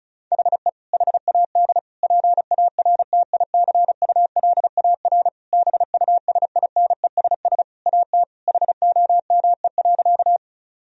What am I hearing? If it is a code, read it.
Morse code, heard as a keyed tone, about 36 wpm, 700 Hz. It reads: HE HAD PARTICULAR BUSINESS AT HOME.
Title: Sentences from Top 1000 Words - Review Part 4 - 35wpm